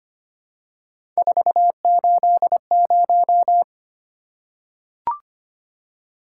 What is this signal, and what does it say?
Morse code, heard as a keyed tone, about 26 wpm, 700 Hz. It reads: 480
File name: Three Letter-Number Combinations - 25wpm